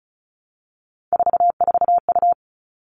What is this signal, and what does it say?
Morse code, heard as a keyed tone, about 34 wpm, 700 Hz. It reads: 44U
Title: All Cut Numbers - 3-Digit 35wpm